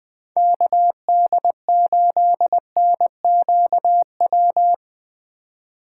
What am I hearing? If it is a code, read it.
Morse code, heard as a keyed tone, about 21 wpm, 700 Hz. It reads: KD8NQW
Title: USA Call Signs - 6 Characters - 3x Character Spacing - 20wpm